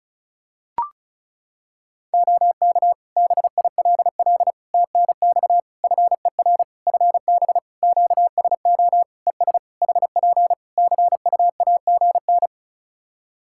Morse code, heard as a keyed tone, about 36 wpm, 700 Hz. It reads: OK BILL TNX FER FB QSO ES HP CUAGN
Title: Practice QSO - Exchange 4 - Ending - 35wpm